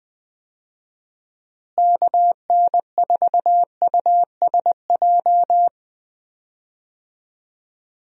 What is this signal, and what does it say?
Morse code, heard as a keyed tone, about 21 wpm, 700 Hz. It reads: KN4USJ
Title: USA Call Signs - 6 Characters - Repeated 3x - 20wpm